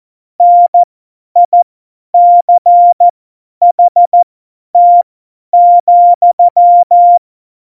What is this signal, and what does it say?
Morse code, heard as a keyed tone, about 14 wpm, spaced as for 10 wpm, 700 Hz. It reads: NICHT,
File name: Morsekurs / Witze / Witz 09. Oktober 2018, 50 F70